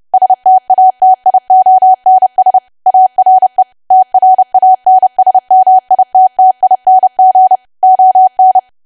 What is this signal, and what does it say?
Morse code, heard as a keyed tone, about 30 wpm, 750 Hz. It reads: STATIONS ARE TRANSMITTING ON
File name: cwa / W1AW-Level3 / 30WPM